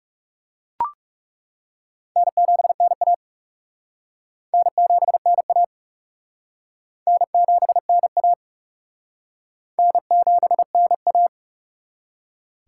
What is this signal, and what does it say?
Morse code, heard as a keyed tone, about 35 wpm, 700 Hz. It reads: N7NA N7NA N7NA N7NA
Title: Speed Racing - USA 1x2 Call Signs 30wpm